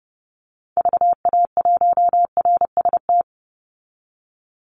Morse code, heard as a keyed tone, about 30 wpm, 700 Hz. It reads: VA1RST
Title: Call Signs Repeated 3x - 30wpm